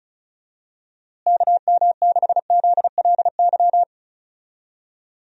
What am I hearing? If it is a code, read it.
Morse code, heard as a keyed tone, about 36 wpm, 700 Hz. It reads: KM6ZLY
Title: USA Call Signs - 6 Characters - Repeated 2x - 35wpm